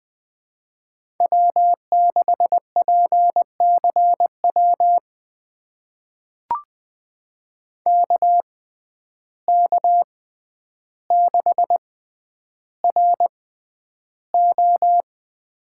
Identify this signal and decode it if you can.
Morse code, heard as a keyed tone, about 20 wpm, spaced as for 10 wpm, 700 Hz. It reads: W6PCW KK6RO
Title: USA Call Signs - 5 Characters - 6x Character Spacing - 20wpm